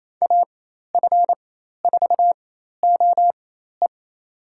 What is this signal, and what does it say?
Morse code, heard as a keyed tone, about 28 wpm, spaced as for 13 wpm, 700 Hz. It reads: AF4OE
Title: USA Call Signs - 4x Character Spacing - 28wpm